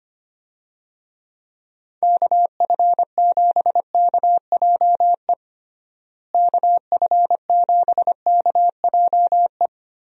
Morse code, heard as a keyed tone, about 25 wpm, 700 Hz. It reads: KF7KJE KF7KJE
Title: USA Call Signs - 6 Characters - Repeated 2x - 25wpm